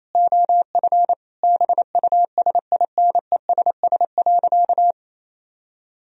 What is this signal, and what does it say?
Morse code, heard as a keyed tone, about 28 wpm, 700 Hz. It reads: OF BUSINESS.
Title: Sentences from Top 1000 Words - Review Part 4 - 28wpm